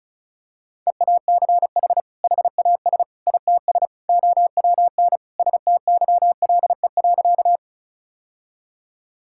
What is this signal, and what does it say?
Morse code, heard as a keyed tone, about 34 wpm, 700 Hz. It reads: EACH HAS ITS OWN STYLE.
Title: Sentences from Top 1000 Words - Review Part 5 - 35wpm